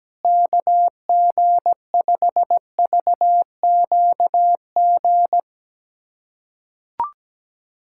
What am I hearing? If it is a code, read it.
Morse code, heard as a keyed tone, about 17 wpm, 700 Hz. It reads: KG5VQG
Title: USA Callsign DE Callsign 17wpm